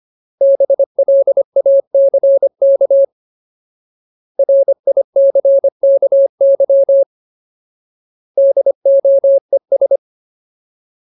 Morse code, heard as a keyed tone, about 25 wpm, 550 Hz. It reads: BLACK RICKY DOES
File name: Ultimate Word List / 25WPM Ultimate